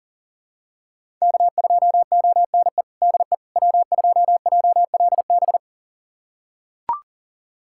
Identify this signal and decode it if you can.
Morse code, heard as a keyed tone, about 40 wpm, 700 Hz. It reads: K2ONE DE W2JLB
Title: USA Callsign DE Callsign - 5 Characters - 40wpm